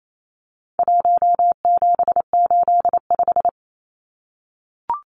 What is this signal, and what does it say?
Morse code, heard as a keyed tone, about 28 wpm, 700 Hz. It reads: 1785
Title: Numbers - 4-Digit 28wpm